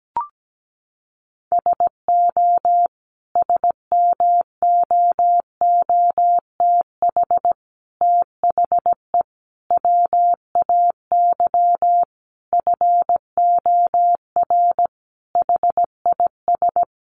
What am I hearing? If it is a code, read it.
Morse code, heard as a keyed tone, about 17 wpm, 700 Hz. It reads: SO SMOOTH THE WAY FOR HIS
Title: Sentences from Top 1400 Words 17wpm